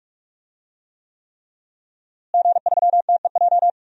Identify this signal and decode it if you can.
Morse code, heard as a keyed tone, about 46 wpm, 700 Hz. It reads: G3TEJ
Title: DXCC-100 Call Signs - 5 Characters - Repeated 3x - 45wpm